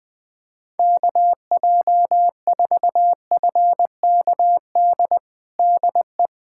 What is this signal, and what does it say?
Morse code, heard as a keyed tone, about 21 wpm, 690 Hz. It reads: KJ4FKD DE
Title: USA Callsign DE Callsign 20wpm